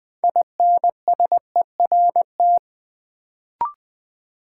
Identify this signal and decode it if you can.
Morse code, heard as a keyed tone, about 20 wpm, 700 Hz. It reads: INSERT
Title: Build A Word - 2 Letters At A Time - 6-Letter Words - 20wpm